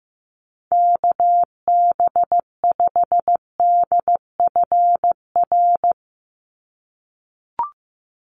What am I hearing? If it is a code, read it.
Morse code, heard as a keyed tone, about 15 wpm, 700 Hz. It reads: KB5DFR
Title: USA Call Signs 15wpm